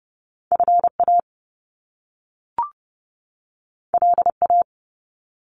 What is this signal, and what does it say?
Morse code, heard as a keyed tone, about 30 wpm, 700 Hz. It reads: FA LA
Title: Top Two Letter Combinations in Words 30wpm